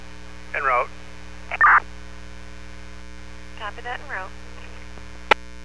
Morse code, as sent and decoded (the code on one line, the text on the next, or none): none